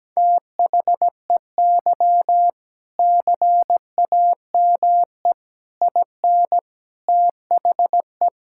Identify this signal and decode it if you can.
Morse code, heard as a keyed tone, about 17 wpm, 700 Hz. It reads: THEY CAME IN THE
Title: Sentences from Top 1000 Words - Review Part 6 - 17wpm